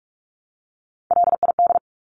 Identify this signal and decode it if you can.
Morse code, heard as a keyed tone, about 46 wpm, 700 Hz. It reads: LID
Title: Abbreviations Repeated 3x - 45wpm